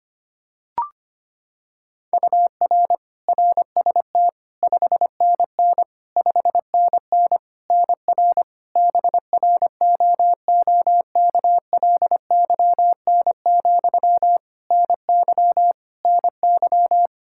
Morse code, heard as a keyed tone, about 25 wpm, 700 Hz. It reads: UR RST 5NN 5NN NR BROOKLYN, NY NY
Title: K1USN SST QSO Practice 25wpm